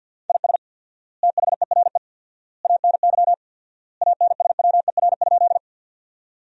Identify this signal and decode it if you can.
Morse code, heard as a keyed tone, about 50 wpm, 700 Hz. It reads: IS THERE ANY ANSWER?
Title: 4x Word-Spacing - Sentences from Top 300 Words - 50wpm